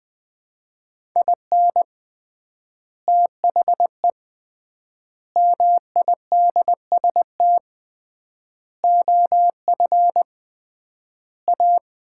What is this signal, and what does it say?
Morse code, heard as a keyed tone, about 20 wpm, 700 Hz. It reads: IN THE MIDST OF A
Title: 3x Word-Spacing - Sets of 5 Words - 20wpm